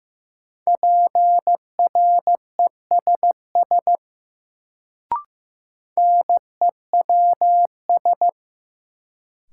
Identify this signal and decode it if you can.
Morse code, heard as a keyed tone, about 15 wpm, 700 Hz. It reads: PRESS NEWS
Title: Sentences from Top 800 Words - New Word Drill - 15wpm